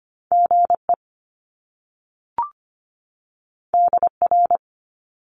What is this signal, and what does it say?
Morse code, heard as a keyed tone, about 25 wpm, 700 Hz. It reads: GE DR
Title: Top Two Letter Combinations in Words 25wpm